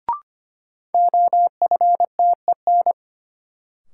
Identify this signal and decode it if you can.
Morse code, heard as a keyed tone, about 25 wpm, 700 Hz. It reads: OFTEN
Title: Sentences from Top 300 Words - New Word Drill - 25wpm